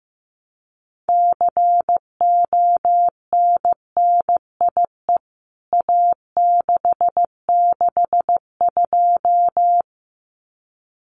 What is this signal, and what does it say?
Morse code, heard as a keyed tone, about 15 wpm, 700 Hz. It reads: CONNIE A662
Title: CWT Exchange 15wpm